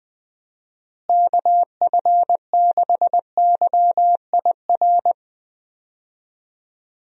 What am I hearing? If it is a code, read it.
Morse code, heard as a keyed tone, about 20 wpm, 700 Hz. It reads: KF6YIR
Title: USA Call Signs - 6 Characters - Repeated 2x - 20wpm